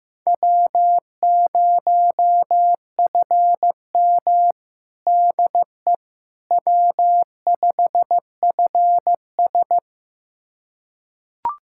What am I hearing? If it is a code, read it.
Morse code, heard as a keyed tone, about 15 wpm, 700 Hz. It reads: W0FM DE W5FS
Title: USA Callsign DE Callsign - 4 Characters - 15wpm